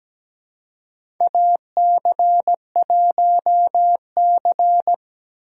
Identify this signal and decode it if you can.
Morse code, heard as a keyed tone, about 17 wpm, 700 Hz. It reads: AC1C
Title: USA Call Signs - 3x Character Spacing - 17wpm